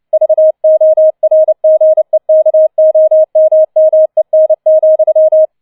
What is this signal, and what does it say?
Morse code, heard as a keyed tone, about 29 wpm, 630 Hz. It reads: VORGEKOMMEN,